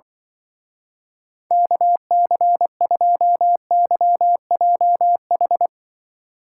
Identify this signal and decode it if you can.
Morse code, heard as a keyed tone, about 24 wpm, 700 Hz. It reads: KC2YJH
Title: Speed Racing - USA 2x3 Call Signs 20wpm